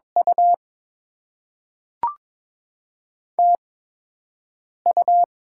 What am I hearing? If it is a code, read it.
Morse code, heard as a keyed tone, about 22 wpm, spaced as for 6 wpm, 700 Hz. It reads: U TU
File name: Top 20 QSO Elements - 8x Character Spacing - 22wpm